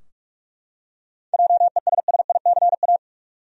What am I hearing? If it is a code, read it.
Morse code, heard as a keyed tone, about 45 wpm, 700 Hz. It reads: JESSICA
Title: USA Common Names - 45wpm